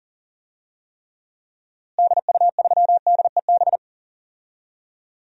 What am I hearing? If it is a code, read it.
Morse code, heard as a keyed tone, about 40 wpm, 700 Hz. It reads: DU3DEB
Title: DXCC-100 Call Signs - 6 Characters - Repeated 3x - 40wpm